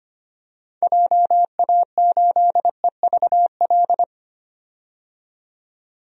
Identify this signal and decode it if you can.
Morse code, heard as a keyed tone, about 25 wpm, 700 Hz. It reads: JA8EVL
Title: DXCC-100 Call Signs - 6 Characters - 25wpm